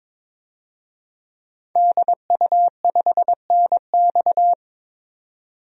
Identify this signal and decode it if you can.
Morse code, heard as a keyed tone, about 22 wpm, 700 Hz. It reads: DU5NX
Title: DXCC-100 Call Signs - 5 Characters - 22wpm